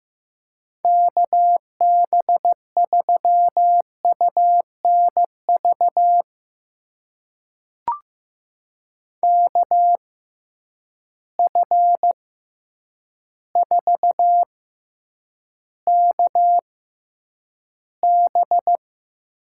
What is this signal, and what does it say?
Morse code, heard as a keyed tone, about 15 wpm, spaced as for 8 wpm, 700 Hz. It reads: KB3UNV KF4KB
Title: USA Call Signs - 6 Characters - 6x Character Spacing - 15wpm